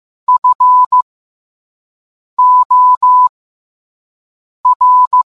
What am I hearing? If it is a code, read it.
Morse code, heard as a keyed tone, about 15 wpm, spaced as for 5 wpm, 1000 Hz. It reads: FOR